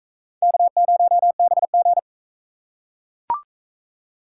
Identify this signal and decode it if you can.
Morse code, heard as a keyed tone, about 44 wpm, 700 Hz. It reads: K0BG